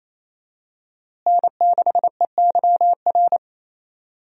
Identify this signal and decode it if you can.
Morse code, heard as a keyed tone, about 28 wpm, 700 Hz. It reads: N6EYR
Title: USA Call Signs - 5 Characters - Repeated 2x - 28wpm